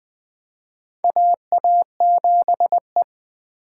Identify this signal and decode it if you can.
Morse code, heard as a keyed tone, about 20 wpm, 700 Hz. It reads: AA7E
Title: USA Call Signs - 3x Character Spacing - 20wpm